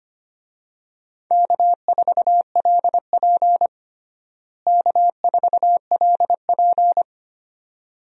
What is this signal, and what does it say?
Morse code, heard as a keyed tone, about 25 wpm, 700 Hz. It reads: K4LP K4LP
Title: USA Call Signs - 4 Characters - Repeated 2x - 25wpm